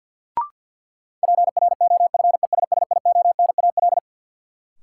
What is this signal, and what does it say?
Morse code, heard as a keyed tone, about 53 wpm, 700 Hz. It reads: PROFESSIONAL
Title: Sentences from Top 1400 Words - New Word Drill Alt - 50wpm